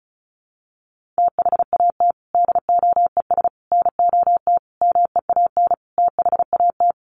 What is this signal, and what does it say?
Morse code, heard as a keyed tone, about 34 wpm, 700 Hz. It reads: THAT DOES NOT MEAN THAT
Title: Sets of 5 Words Encore 35wpm